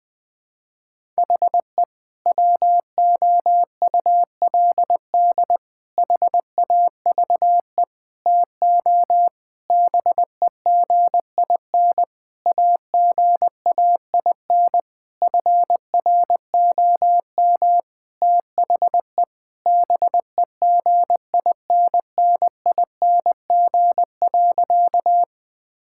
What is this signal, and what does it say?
Morse code, heard as a keyed tone, about 20 wpm, 700 Hz. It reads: HE WOULD HAVE TO BEGIN AGAIN FROM THE BEGINNING.
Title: Sentences from Top 1000 Words - Review Part 5 - 20wpm